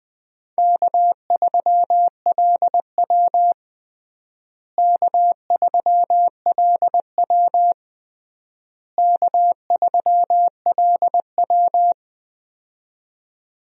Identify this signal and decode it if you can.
Morse code, heard as a keyed tone, about 20 wpm, 700 Hz. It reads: K3LW K3LW K3LW
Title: USA Call Signs - 4 Characters - Repeated 3x - 20wpm